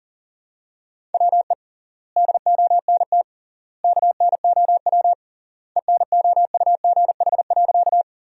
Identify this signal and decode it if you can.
Morse code, heard as a keyed tone, about 40 wpm, 700 Hz. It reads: WE DONT KNOW ENOUGH.
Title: 3x Word-Spacing - Sentences from Top 400 Words - 40wpm